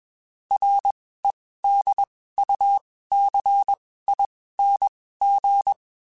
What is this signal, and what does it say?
Morse code, heard as a keyed tone, about 21 wpm, spaced as for 15 wpm, 800 Hz. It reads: REDUCING
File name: QSOs / QSOs-mp3s / WD0FBT 1@20